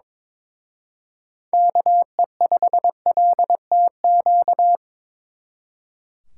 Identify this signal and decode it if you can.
Morse code, heard as a keyed tone, about 26 wpm, 700 Hz. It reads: KE5LTQ
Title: Speed Racing - USA 2x3 Call Signs 22wpm